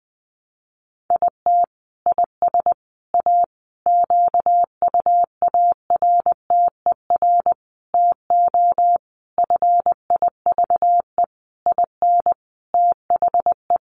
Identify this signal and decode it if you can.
Morse code, heard as a keyed tone, about 20 wpm, 700 Hz. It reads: IT IS A QUARTER TO FIVE IN THE
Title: Sentences from Top 1300 Words 20wpm